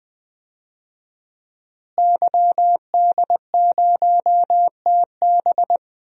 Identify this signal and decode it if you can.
Morse code, heard as a keyed tone, about 21 wpm, 700 Hz. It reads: YD0TB
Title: DXCC-100 Call Signs - 5 Characters - Repeated 2x - 20wpm